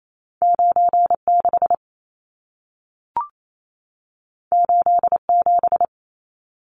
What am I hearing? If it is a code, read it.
Morse code, heard as a keyed tone, about 28 wpm, 700 Hz. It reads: 96 87
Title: Numbers - 2-Digit 28wpm